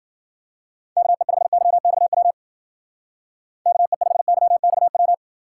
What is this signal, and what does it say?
Morse code, heard as a keyed tone, about 51 wpm, 700 Hz. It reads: KE5YXW KE5YXW